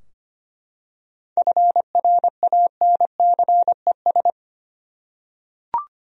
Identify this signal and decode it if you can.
Morse code, heard as a keyed tone, about 25 wpm, 700 Hz. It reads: FRANCES
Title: USA Common Names - 25wpm